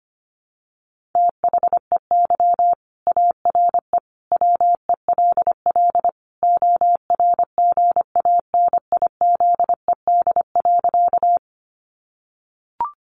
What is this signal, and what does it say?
Morse code, heard as a keyed tone, about 25 wpm, 700 Hz. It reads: THEY ARE WELL ORGANIZED.
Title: Sentences from Top 1300 Words 25wpm